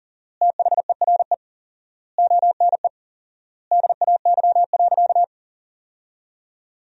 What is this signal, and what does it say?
Morse code, heard as a keyed tone, about 40 wpm, 700 Hz. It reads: THERE ONE DAY.
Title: 4x Word-Spacing - Sentences from Top 300 Words - 40wpm